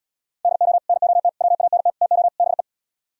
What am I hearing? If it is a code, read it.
Morse code, heard as a keyed tone, about 77 wpm, 700 Hz. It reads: DO NOT WANT TO BE